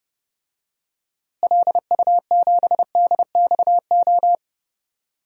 Morse code, heard as a keyed tone, about 30 wpm, 700 Hz. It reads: LU7DXO
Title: DXCC-100 Call Signs - 6 Characters - Repeated 2x - 30wpm